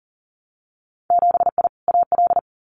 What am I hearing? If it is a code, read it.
Morse code, heard as a keyed tone, about 40 wpm, 700 Hz. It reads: 7I AL
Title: Winter Field Day Exchanges Encore 40wpm